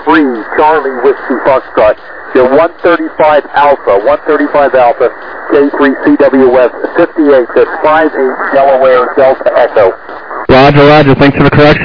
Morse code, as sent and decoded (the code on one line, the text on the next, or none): none